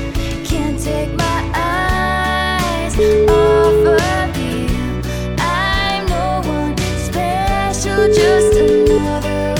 Ringback tone (Europe, 425 Hz), with noise.